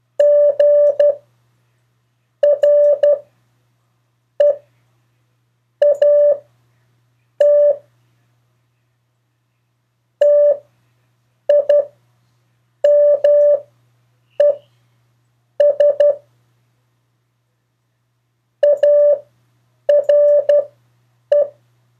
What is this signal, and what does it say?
Morse code, heard as a keyed tone, about 12 wpm, spaced as for 6 wpm, 580 Hz. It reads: GREAT TIMES ARE
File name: 4of4-sentences-mistakes-test